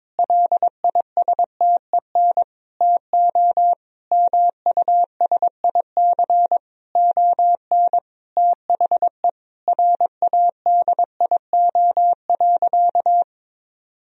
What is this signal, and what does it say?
Morse code, heard as a keyed tone, about 22 wpm, 700 Hz. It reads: LISTEN TO MUSIC ON THE RADIO.
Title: Sentences from Top 1000 Words - Review Part 3 - 22wpm